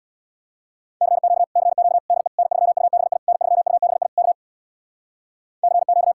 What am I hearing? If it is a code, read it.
Morse code, heard as a keyed tone, about 64 wpm, 700 Hz. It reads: CQ CQ DE N3UBI N3UBI K CQ